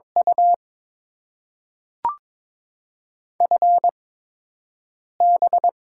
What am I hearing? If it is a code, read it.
Morse code, heard as a keyed tone, about 22 wpm, spaced as for 6 wpm, 700 Hz. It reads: U FB
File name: Top 90 QSO Elements - 8x Character Spacing - 22wpm